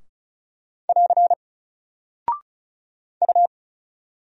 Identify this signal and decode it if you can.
Morse code, heard as a keyed tone, about 35 wpm, 700 Hz. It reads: <AR> U
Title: Single Character (Letter:Number:Punctuation:Prosign) 35wpm